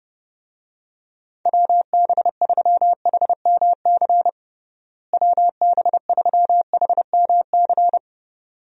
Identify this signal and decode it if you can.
Morse code, heard as a keyed tone, about 31 wpm, 700 Hz. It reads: WB3HMC WB3HMC
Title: USA Call Signs - 6 Characters - Repeated 2x - 30wpm